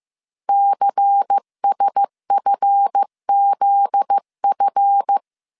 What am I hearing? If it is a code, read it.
Morse code, heard as a keyed tone, about 15 wpm, 800 Hz. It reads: CSFZF
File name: CW / DET / 1 9 1500C S A 14wpm 3222